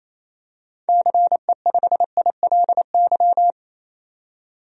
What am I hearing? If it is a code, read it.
Morse code, heard as a keyed tone, about 28 wpm, 700 Hz. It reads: CE5ILY
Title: DXCC-100 Call Signs - 6 Characters - 28wpm